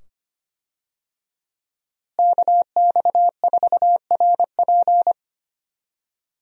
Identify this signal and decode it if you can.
Morse code, heard as a keyed tone, about 25 wpm, 700 Hz. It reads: KX4RP
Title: USA Call Signs - 5 Characters - Repeated 2x - 25wpm